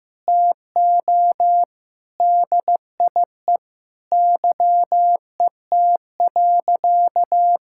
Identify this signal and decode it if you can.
Morse code, heard as a keyed tone, about 15 wpm, 700 Hz. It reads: TO DIE YET.
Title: Sentences from Top 1600 Words 15wpm